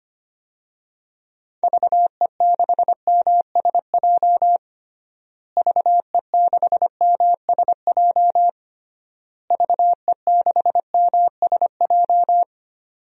Morse code, heard as a keyed tone, about 25 wpm, 700 Hz. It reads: VE6MSJ VE6MSJ VE6MSJ
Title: DXCC-100 Call Signs - 6 Characters - Repeated 3x - 25wpm